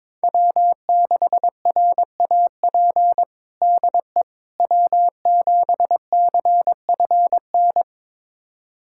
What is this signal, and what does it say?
Morse code, heard as a keyed tone, about 22 wpm, 700 Hz. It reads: W6RAP DE W7CFN
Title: USA Callsign DE Callsign - 5 Characters - 22wpm